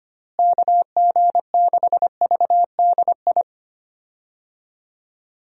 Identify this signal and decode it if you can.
Morse code, heard as a keyed tone, about 25 wpm, 700 Hz. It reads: KG6VDI
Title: USA Call Signs - 6 Characters - Repeated 3x - 25wpm